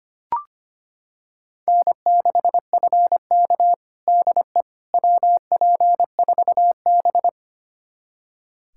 Morse code, heard as a keyed tone, about 25 wpm, 700 Hz. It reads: N6FK DE WP4B
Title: USA Callsign DE Callsign - 4 Characters - 25wpm